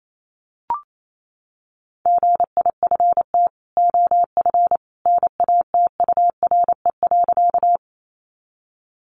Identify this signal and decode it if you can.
Morse code, heard as a keyed tone, about 28 wpm, 700 Hz. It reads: GIFT OF NATURE.
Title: Sentences from Top 900 Words 28wpm